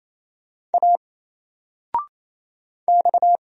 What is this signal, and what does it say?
Morse code, heard as a keyed tone, about 27 wpm, 700 Hz. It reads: A X
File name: Single-Letter-Number-28wpm